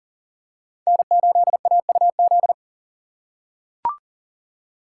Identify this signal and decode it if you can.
Morse code, heard as a keyed tone, about 40 wpm, 700 Hz. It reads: N8AUZ
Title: USA Call Signs - 5 Characters - 4x Character Spacing - 40wpm